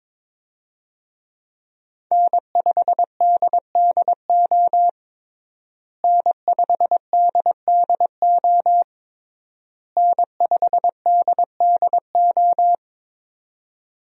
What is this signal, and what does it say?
Morse code, heard as a keyed tone, about 22 wpm, 700 Hz. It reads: N5DDO N5DDO N5DDO
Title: USA Call Signs - 5 Characters - Repeated 3x - 22wpm